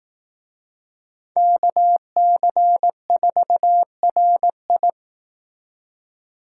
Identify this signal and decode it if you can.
Morse code, heard as a keyed tone, about 18 wpm, 700 Hz. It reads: KC4RI
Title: Speed Racing - USA 2x2 Call Signs 15wpm